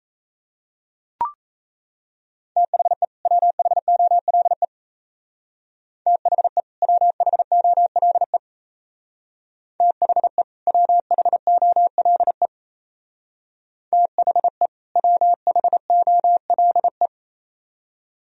Morse code, heard as a keyed tone, about 34 wpm, 700 Hz. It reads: THE WHOLE THE WHOLE THE WHOLE THE WHOLE
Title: Speed-Racing - Sets of 2 Words 28wpm